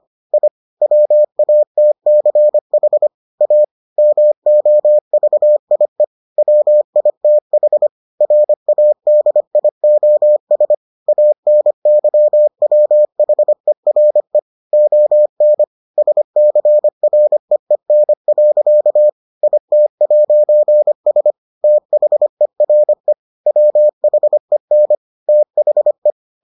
Morse code, heard as a keyed tone, about 25 wpm, 600 Hz. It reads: I WATCH A MOVIE WITH RADIOS ANYWHERE ON SCREEN. IT'S THERE WHEN THE